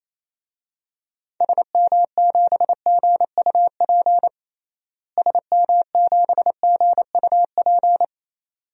Morse code, heard as a keyed tone, about 28 wpm, 700 Hz. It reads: SM7GUP SM7GUP
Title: DXCC-100 Call Signs - 6 Characters - Repeated 2x - 28wpm